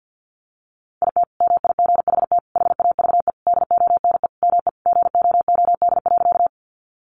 Morse code, heard as a keyed tone, about 50 wpm, 700 Hz. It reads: IT MIGHT HAVE DONE ME GOOD.
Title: Sentences from Top 500 Words 50wpm